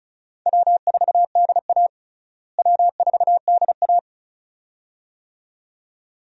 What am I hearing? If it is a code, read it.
Morse code, heard as a keyed tone, about 36 wpm, 700 Hz. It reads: W4DA W4DA
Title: USA Call Signs - 4 Characters - Repeated 2x - 35wpm